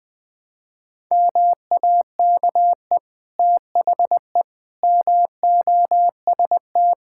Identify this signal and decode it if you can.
Morse code, heard as a keyed tone, about 20 wpm, 700 Hz. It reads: MAKE THE MOST
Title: Difficult Sentences with Hints - 20wpm